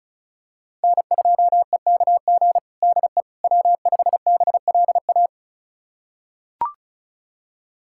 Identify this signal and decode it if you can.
Morse code, heard as a keyed tone, about 36 wpm, 700 Hz. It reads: N2EKG DE W5BLA
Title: USA Callsign DE Callsign - 5 Characters - 35wpm